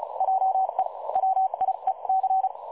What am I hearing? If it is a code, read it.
Morse code, heard as a keyed tone, about 36 wpm, spaced as for 30 wpm, 760 Hz. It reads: JE WIEC
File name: dlazaawansowanych / nagraniazpasma / websdr recording 3565.0kHz